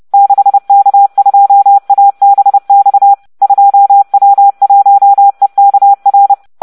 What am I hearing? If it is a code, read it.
Morse code, heard as a keyed tone, about 30 wpm, 800 Hz. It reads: 6K2ABX 2W1EKR